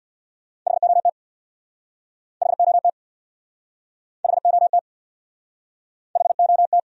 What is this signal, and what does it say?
Morse code, heard as a keyed tone, about 68 wpm, 700 Hz. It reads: HOT HOT HOT HOT